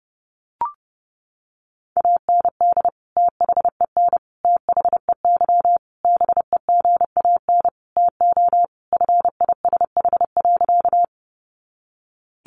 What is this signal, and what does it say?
Morse code, heard as a keyed tone, about 30 wpm, 700 Hz. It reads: AND THEN THEY BEGAN TO FISH.
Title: Sentences from Top 300 Words 30wpm